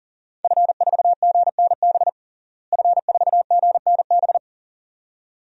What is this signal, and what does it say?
Morse code, heard as a keyed tone, about 40 wpm, 700 Hz. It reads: F4GNB F4GNB
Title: DXCC-100 Call Signs - 5 Characters - Repeated 2x - 40wpm